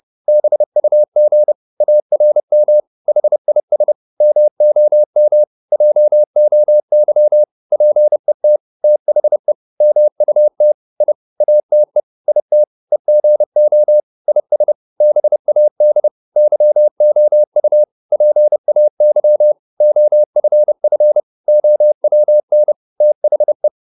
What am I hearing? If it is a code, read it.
Morse code, heard as a keyed tone, about 30 wpm, 600 Hz. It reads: BUG ARM HIS MOM JOY PET THE MUT I ATE IT EGO IS BAD YOU PAY OFF OWN THE